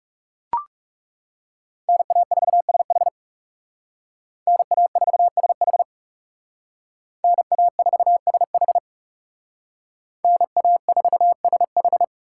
Morse code, heard as a keyed tone, about 39 wpm, 700 Hz. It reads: NA4SH NA4SH NA4SH NA4SH